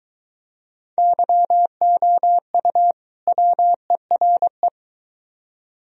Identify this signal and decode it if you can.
Morse code, heard as a keyed tone, about 23 wpm, 700 Hz. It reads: YOU WERE